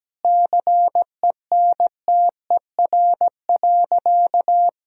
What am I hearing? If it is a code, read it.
Morse code, heard as a keyed tone, about 17 wpm, 700 Hz. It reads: CENTER.
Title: Sentences from Top 1000 Words - Review Part 2 - 17wpm